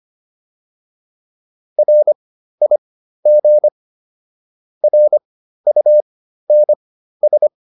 Morse code, heard as a keyed tone, about 25 wpm, spaced as for 13 wpm, 600 Hz. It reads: RIG RUNS
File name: QSO202 13